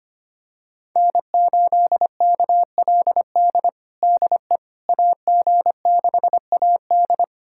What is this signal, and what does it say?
Morse code, heard as a keyed tone, about 25 wpm, 700 Hz. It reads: N8KLD DE AG6AD
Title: USA Callsign DE Callsign - 5 Characters - 25wpm